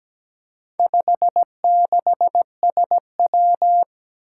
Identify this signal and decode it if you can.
Morse code, heard as a keyed tone, about 17 wpm, 700 Hz. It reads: 56SW
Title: Four Challenging Letter-Number Combinations 17wpm